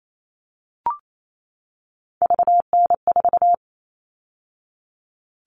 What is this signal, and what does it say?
Morse code, heard as a keyed tone, about 28 wpm, 700 Hz. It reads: VN4
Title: All Cut Numbers - 3-Digit 28wpm